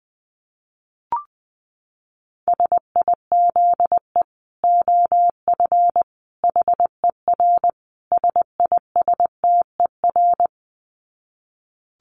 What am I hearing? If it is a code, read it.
Morse code, heard as a keyed tone, about 20 wpm, 700 Hz. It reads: SIZE OF HER SISTER
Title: Dit Challenge 20wpm